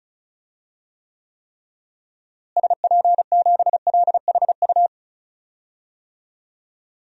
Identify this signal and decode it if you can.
Morse code, heard as a keyed tone, about 36 wpm, 700 Hz. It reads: SP7LHU
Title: DXCC-100 Call Signs - 6 Characters - Repeated 3x - 35wpm